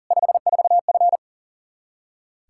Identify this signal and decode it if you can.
Morse code, heard as a keyed tone, about 40 wpm, 700 Hz. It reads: 54F
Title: Practice QSO - Exchange 2 - Info - 40wpm